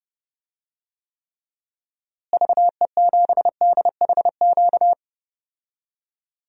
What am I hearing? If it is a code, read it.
Morse code, heard as a keyed tone, about 31 wpm, 700 Hz. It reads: VE7DHQ
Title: DXCC-100 Call Signs - 6 Characters - Repeated 3x - 30wpm